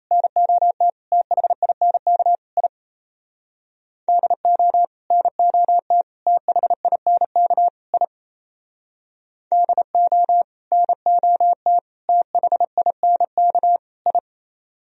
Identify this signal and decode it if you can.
Morse code, heard as a keyed tone, about 37 wpm, 700 Hz. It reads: NOT THINK I DO NOT THINK I DO NOT THINK I